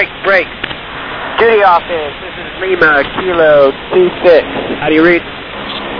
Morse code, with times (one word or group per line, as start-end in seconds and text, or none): none